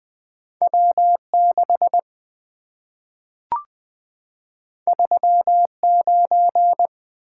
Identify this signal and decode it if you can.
Morse code, heard as a keyed tone, about 20 wpm, 700 Hz. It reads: W6 39
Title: Two Letter-Number Combinations - 20wpm